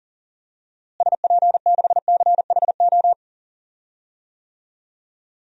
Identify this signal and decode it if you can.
Morse code, heard as a keyed tone, about 40 wpm, 700 Hz. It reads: SP6CHO
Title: DXCC-100 Call Signs - 6 Characters - Repeated 3x - 40wpm